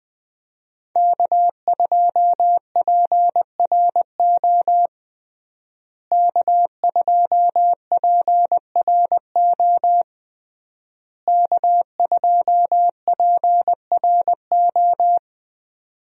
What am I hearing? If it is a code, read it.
Morse code, heard as a keyed tone, about 20 wpm, 700 Hz. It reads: K2PRO K2PRO K2PRO
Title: USA Call Signs - 5 Characters - Repeated 3x - 20wpm